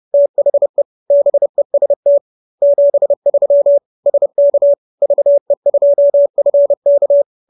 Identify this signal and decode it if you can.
Morse code, heard as a keyed tone, about 30 wpm, 570 Hz. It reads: THE BEST 73 SK VE2FK